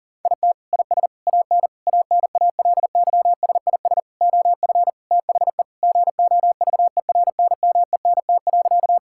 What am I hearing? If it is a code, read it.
Morse code, heard as a keyed tone, about 40 wpm, 700 Hz. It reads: IT IS AN ANALYSIS OF THE GOVERNMENT.
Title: Sentences from Top 1000 Words - Review Part 5 - 40wpm